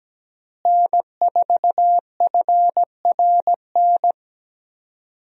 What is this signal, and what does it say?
Morse code, heard as a keyed tone, about 17 wpm, 700 Hz. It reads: N4FRN
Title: USA Call Signs - 5 Characters - Repeated 3x - 17wpm